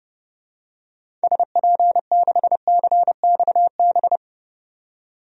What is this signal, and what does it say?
Morse code, heard as a keyed tone, about 31 wpm, 700 Hz. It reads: SP6CXB
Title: DXCC-100 Call Signs - 6 Characters - 30wpm